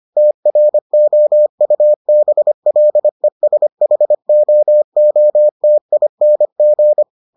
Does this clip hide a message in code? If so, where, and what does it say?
Yes: Morse code from 0.2 to 7.0 s, heard as a keyed tone, about 25 wpm, 600 Hz. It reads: TROUBLESHOOTING